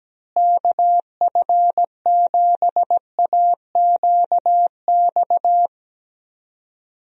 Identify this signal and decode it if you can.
Morse code, heard as a keyed tone, about 17 wpm, 700 Hz. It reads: KF7AQX
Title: USA Call Signs - 6 Characters - Repeated 3x - 17wpm